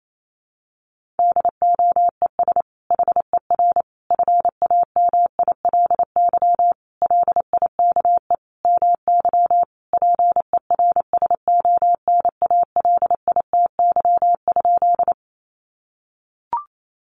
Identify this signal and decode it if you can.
Morse code, heard as a keyed tone, about 28 wpm, 700 Hz. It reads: DOES HER FAMILY LIKE MY PERSONALITY?
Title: Sentences from Top 1400 Words 28wpm